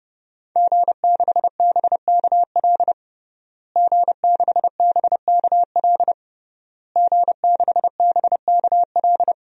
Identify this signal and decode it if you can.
Morse code, heard as a keyed tone, about 32 wpm, 700 Hz. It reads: G6BKL G6BKL G6BKL
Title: DXCC-100 Call Signs - 5 Characters - Repeated 3x - 30wpm